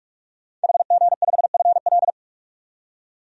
Morse code, heard as a keyed tone, about 46 wpm, 700 Hz. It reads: HG5FL
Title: DXCC-100 Call Signs - 5 Characters - 45wpm